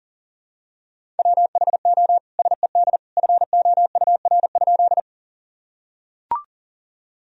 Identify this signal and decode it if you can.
Morse code, heard as a keyed tone, about 40 wpm, 700 Hz. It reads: WHO SED FOUR?
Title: Sentences from Top 300 Words and Abbreviations 40wpm